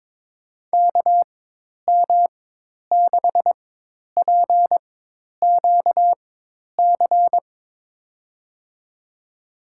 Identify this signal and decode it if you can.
Morse code, heard as a keyed tone, about 22 wpm, spaced as for 10 wpm, 700 Hz. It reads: KM6PQC
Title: USA Call Signs - 6 Characters - 4x Character Spacing - 22wpm